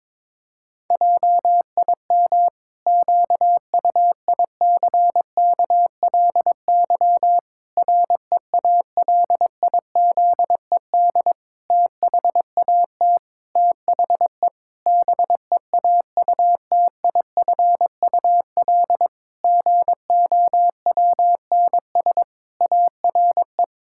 Morse code, heard as a keyed tone, about 22 wpm, 700 Hz. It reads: JIM QUICKLY REALIZED THAT THE BEAUTIFUL GOWNS ARE
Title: Pangram Challenge v2 - 22wpm